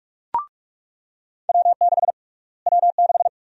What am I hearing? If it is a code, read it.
Morse code, heard as a keyed tone, about 46 wpm, 700 Hz. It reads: W6 W6
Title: USA Call Sign Prefixes - Repeated 2x - 45wpm